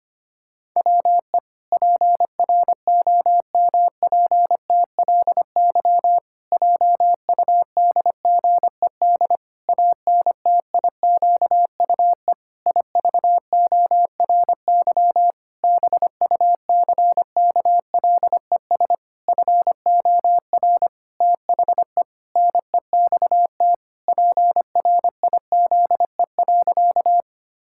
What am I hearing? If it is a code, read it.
Morse code, heard as a keyed tone, about 25 wpm, 700 Hz. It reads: WE PROMPTLY JUDGED ANTIQUE IVORY BUCKLES FOR THE NEXT PRIZE.
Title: Pangram Challenge v2 - 25wpm